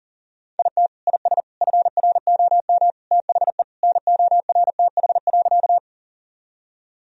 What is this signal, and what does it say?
Morse code, heard as a keyed tone, about 40 wpm, 700 Hz. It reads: IT IS FROM THE NORTH.
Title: Sentences from Top 1000 Words - Review Part 1 - 40wpm